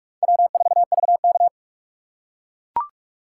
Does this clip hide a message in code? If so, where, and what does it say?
Yes: Morse code from 0.2 to 1.5 s, heard as a keyed tone, about 42 wpm, 700 Hz. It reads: W4VK